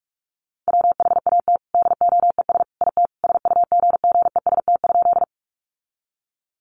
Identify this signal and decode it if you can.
Morse code, heard as a keyed tone, about 44 wpm, 700 Hz. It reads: WHAT DOES IT SUGGEST?
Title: Sentences from Top 1300 Words 45wpm